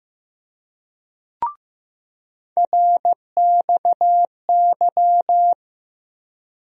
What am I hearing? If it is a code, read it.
Morse code, heard as a keyed tone, about 15 wpm, 700 Hz. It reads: RXY
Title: USA Call Sign Postfixes - 3 Letters 15wpm